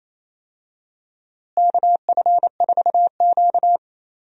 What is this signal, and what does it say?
Morse code, heard as a keyed tone, about 28 wpm, 700 Hz. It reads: KF4Q
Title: USA Call Signs - 4 Characters - Repeated 2x - 28wpm